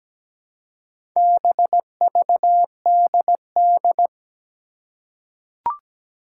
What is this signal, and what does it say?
Morse code, heard as a keyed tone, about 17 wpm, 700 Hz. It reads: BVDD
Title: Four Challenging Letter-Number Combinations 17wpm